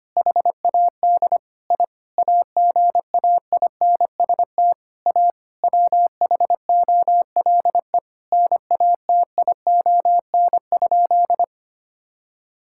Morse code, heard as a keyed tone, about 25 wpm, 700 Hz. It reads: HAD I AGAINST A WHOLE NATION?
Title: Sentences from Top 1000 Words - Review Part 6 - 25wpm